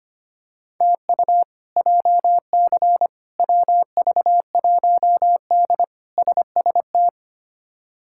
Morse code, heard as a keyed tone, about 25 wpm, 700 Hz. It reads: TU JC WV1D SST
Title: K1USN SST QSO Practice - 25wpm